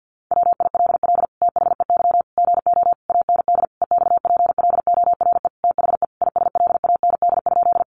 Morse code, heard as a keyed tone, about 50 wpm, 700 Hz. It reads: WILL THEY GO AND EXPLORE THE ISLAND?
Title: Sentences from Top 1100 Words 50wpm